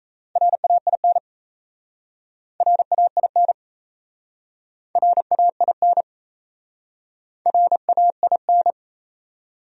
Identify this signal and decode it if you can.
Morse code, heard as a keyed tone, about 39 wpm, 700 Hz. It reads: RAIN RAIN RAIN RAIN